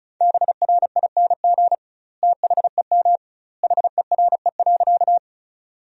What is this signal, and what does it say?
Morse code, heard as a keyed tone, about 36 wpm, 700 Hz. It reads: BRING THEM HERE.
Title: 2x Word-Spacing - Sentences from Top 500 Words - 35wpm